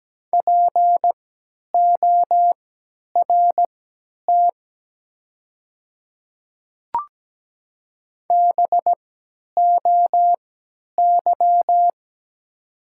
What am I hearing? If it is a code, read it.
Morse code, heard as a keyed tone, about 17 wpm, spaced as for 10 wpm, 700 Hz. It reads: PORT BOY
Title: 3x Character Spacing - Top 300 Words - 17wpm